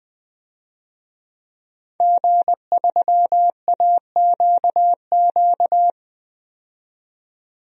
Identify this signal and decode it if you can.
Morse code, heard as a keyed tone, about 21 wpm, 700 Hz. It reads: G3AQQ
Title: DXCC-100 Call Signs - 5 Characters - 20wpm